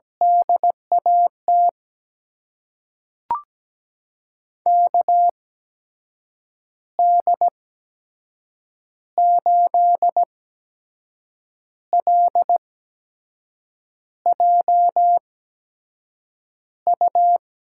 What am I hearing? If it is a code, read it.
Morse code, heard as a keyed tone, about 17 wpm, spaced as for 6 wpm, 700 Hz. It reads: DAT KD8LJU
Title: USA Call Signs - 6 Characters - 8x Character Spacing - 17wpm